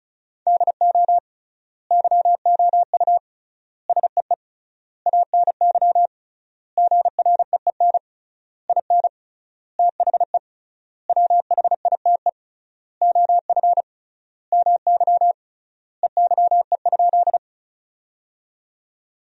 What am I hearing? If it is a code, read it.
Morse code, heard as a keyed tone, about 34 wpm, 700 Hz. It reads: DO YOU SEE ANY GREEN IN THE WHITE OF MY EYE?
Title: 3x Word-Spacing - Sentences from Top 500 Words - 35wpm